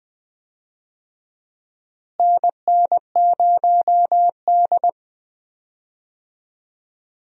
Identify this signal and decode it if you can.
Morse code, heard as a keyed tone, about 20 wpm, 700 Hz. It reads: NN0D
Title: USA Call Signs - 4 Characters - Repeated 2x - 20wpm